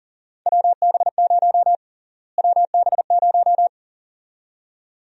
Morse code, heard as a keyed tone, about 40 wpm, 700 Hz. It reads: WB0 WB0
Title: USA Call Sign Prefixes - Repeated 2x - 40wpm